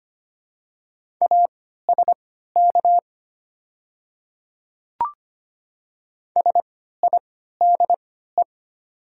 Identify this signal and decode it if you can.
Morse code, heard as a keyed tone, about 25 wpm, spaced as for 14 wpm, 700 Hz. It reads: ASK SIDE
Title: 3x Character Spacing - Top 300 Words - 25wpm